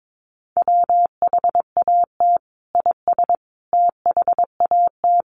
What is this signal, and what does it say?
Morse code, heard as a keyed tone, about 22 wpm, 700 Hz. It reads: WHAT IS THAT
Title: Sentences from Top 1500 Words 22wpm